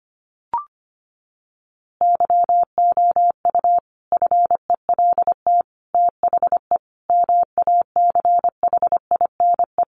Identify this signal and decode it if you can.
Morse code, heard as a keyed tone, about 25 wpm, 700 Hz. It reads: YOU FELT THE MACHINE
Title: A Sound Of Thunder 25wpm